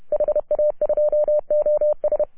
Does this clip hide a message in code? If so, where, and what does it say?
Yes: Morse code from 0.1 to 2.2 s, heard as a keyed tone, about 30 wpm, 600 Hz. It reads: HA2OS